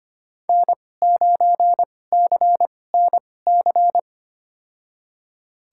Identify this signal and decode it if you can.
Morse code, heard as a keyed tone, about 25 wpm, spaced as for 18 wpm, 700 Hz. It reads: N9CNC
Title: USA Call Signs - 2x Character Spacing - 25wpm